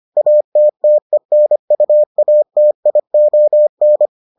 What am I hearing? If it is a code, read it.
Morse code, heard as a keyed tone, about 25 wpm, 600 Hz. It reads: ATTENUATION